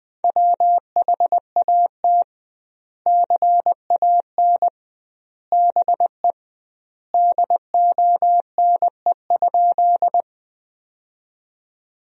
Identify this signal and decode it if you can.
Morse code, heard as a keyed tone, about 20 wpm, 700 Hz. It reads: WHAT CAN BE DONE?
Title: 2x Word-Spacing - Sentences from Top 500 Words - 20wpm